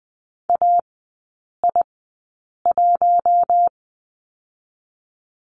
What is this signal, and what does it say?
Morse code, heard as a keyed tone, about 20 wpm, spaced as for 8 wpm, 700 Hz. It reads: AI1
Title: 041 Lesson 8e - Number 1 - Three Characters 20wpm